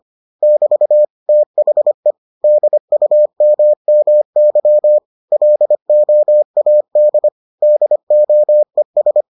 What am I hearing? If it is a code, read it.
Morse code, heard as a keyed tone, about 25 wpm, 600 Hz. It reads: <BT> THE DUMMY LOAD DOES